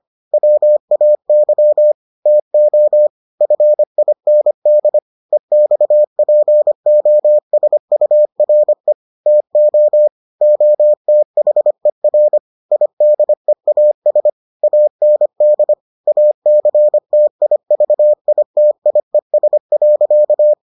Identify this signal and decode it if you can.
Morse code, heard as a keyed tone, about 25 wpm, 600 Hz. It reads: WAY TO FIND EXPOSURE TO OTHER IDEAS AND ACTIVITIES.